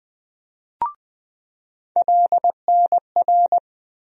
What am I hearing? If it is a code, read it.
Morse code, heard as a keyed tone, about 20 wpm, 700 Hz. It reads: LNR
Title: USA Call Sign Postfixes - 3 Letters 20wpm